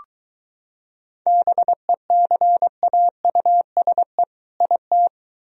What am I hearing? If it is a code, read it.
Morse code, heard as a keyed tone, about 23 wpm, 700 Hz. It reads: BECAUSE IT